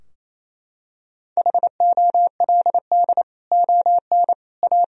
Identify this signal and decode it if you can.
Morse code, heard as a keyed tone, about 28 wpm, 700 Hz. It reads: HOLD ON A
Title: Sentences from Top 1000 Words - Review Part 2 - 28wpm